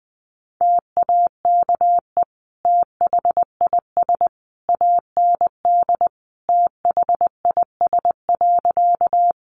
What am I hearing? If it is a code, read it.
Morse code, heard as a keyed tone, about 20 wpm, 700 Hz. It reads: TAKE THIS AND THIS.
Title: Sentences from Top 200 Words 20wpm